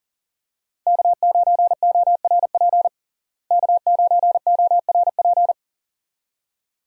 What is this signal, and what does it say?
Morse code, heard as a keyed tone, about 40 wpm, 700 Hz. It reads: K9ORP K9ORP
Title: USA Call Signs - 5 Characters - Repeated 2x - 40wpm